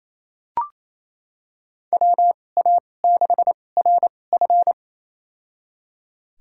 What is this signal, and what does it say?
Morse code, heard as a keyed tone, about 28 wpm, spaced as for 20 wpm, 700 Hz. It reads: WA6RF
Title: USA Call Signs - 5 Characters - 2x Character Spacing - 28wpm